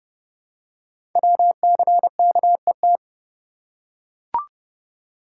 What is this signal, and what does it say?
Morse code, heard as a keyed tone, about 30 wpm, 700 Hz. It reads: WCKET
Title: Code Groups - 5 Letters - 30wpm